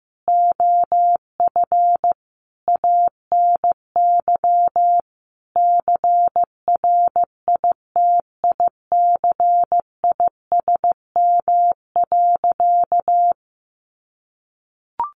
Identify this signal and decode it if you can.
Morse code, heard as a keyed tone, about 15 wpm, 700 Hz. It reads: OF ANY CRITICISM.
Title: Sentences from Top 1300 Words 15wpm